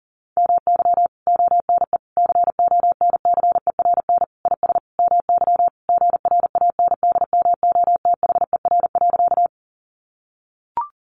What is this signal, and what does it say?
Morse code, heard as a keyed tone, about 40 wpm, 700 Hz. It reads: MY ONE CONCERN IS MY GRANDMOTHER.
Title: Sentences from Top 1200 Words 40wpm